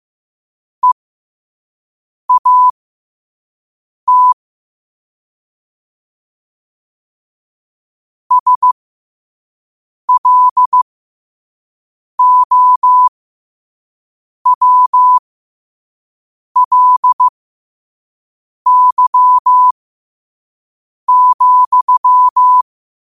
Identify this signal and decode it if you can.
Morse code, heard as a keyed tone, about 15 wpm, spaced as for 5 wpm, 1000 Hz. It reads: EAT SLOWLY,
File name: qotdfiles / 5wpm